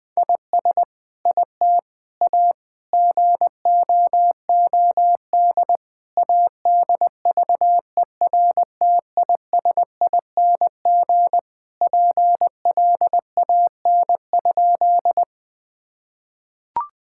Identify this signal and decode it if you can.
Morse code, heard as a keyed tone, about 19 wpm, 700 Hz. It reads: IS IT A GOOD ADVERTISING PLAN?
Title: Sentences from Top 1000 Words - Review Part 6 - 20wpm